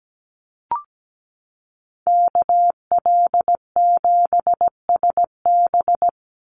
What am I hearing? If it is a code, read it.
Morse code, heard as a keyed tone, about 17 wpm, 700 Hz. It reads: KL7SB
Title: Call Signs Repeated 2x - 17wpm